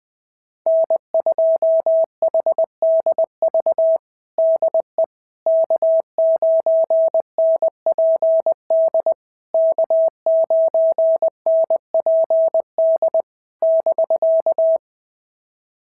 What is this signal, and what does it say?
Morse code, heard as a keyed tone, about 20 wpm, 650 Hz. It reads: N2HDV DE K9NPD K9NPD <BK>
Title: SOTA QSO Practice 20wpm